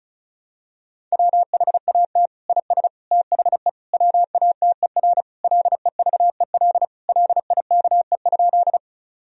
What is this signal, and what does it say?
Morse code, heard as a keyed tone, about 34 wpm, 700 Hz. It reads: WHAT IS THE WATER LEVEL LIKE?
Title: Sentences from Top 1000 Words - Review Part 4 - 35wpm